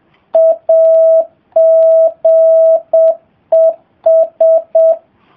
Morse code, heard as a keyed tone, about 7 wpm, 660 Hz. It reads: AGES